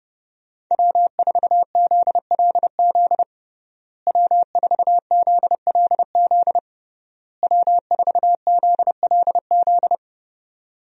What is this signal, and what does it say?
Morse code, heard as a keyed tone, about 32 wpm, 700 Hz. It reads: W4ZLZ W4ZLZ W4ZLZ
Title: USA Call Signs - 5 Characters - Repeated 3x - 30wpm